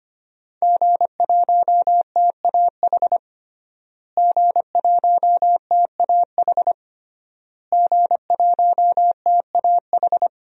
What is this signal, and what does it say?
Morse code, heard as a keyed tone, about 25 wpm, 700 Hz. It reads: G1TAH G1TAH G1TAH
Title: DXCC-100 Call Signs - 5 Characters - Repeated 3x - 25wpm